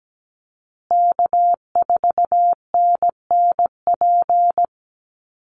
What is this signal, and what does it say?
Morse code, heard as a keyed tone, about 17 wpm, 700 Hz. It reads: K4NNP
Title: USA Call Signs Repeated 3x 17wpm